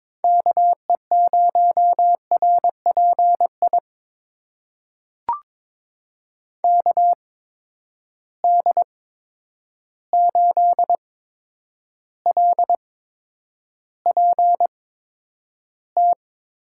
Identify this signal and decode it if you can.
Morse code, heard as a keyed tone, about 22 wpm, spaced as for 9 wpm, 700 Hz. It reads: KE0RPI KD8LPT
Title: USA Call Signs - 6 Characters - 8x Character Spacing - 22wpm